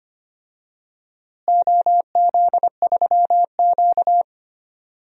Morse code, heard as a keyed tone, about 25 wpm, 700 Hz. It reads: OZ3Q
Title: DXCC-100 Call Signs - 4 Characters - Repeated 3x - 25wpm